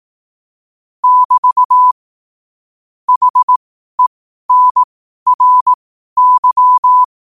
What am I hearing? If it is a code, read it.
Morse code, heard as a keyed tone, about 18 wpm, spaced as for 12 wpm, 1000 Hz. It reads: <BT> HENRY